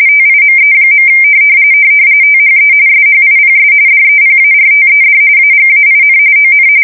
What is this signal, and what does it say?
RTTY, about 45.45 baud. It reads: TIME. THE SHIPS DEPARTED AGAIN ON
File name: fsk170 / USS-Missouri